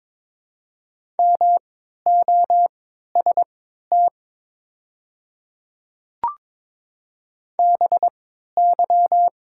Morse code, heard as a keyed tone, about 22 wpm, spaced as for 12 wpm, 700 Hz. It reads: MOST BY
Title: 3x Character Spacing - Top 200 Words - 22wpm